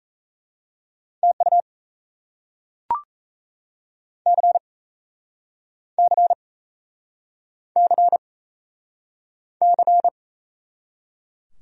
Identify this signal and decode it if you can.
Morse code, heard as a keyed tone, about 40 wpm, 700 Hz. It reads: TU C C C C